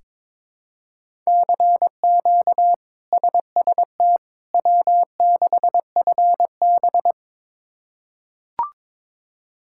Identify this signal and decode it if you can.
Morse code, heard as a keyed tone, about 21 wpm, 700 Hz. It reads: CQ SST W6FB
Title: K1USN SST QSO Practice - 22wpm